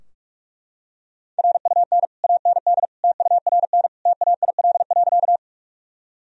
Morse code, heard as a keyed tone, about 41 wpm, 700 Hz. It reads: RUN AND TURN TAIL.
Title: Sentences from Top 1000 Words - Review Part 3 - 45wpm